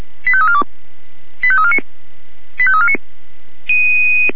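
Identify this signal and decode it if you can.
Five-tone selective call (ZVEI), heard as quick runs of five tones, tones of about 70 ms. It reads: 85342 85349 85349
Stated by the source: ZVEI five-tone call 85340 received by SDR.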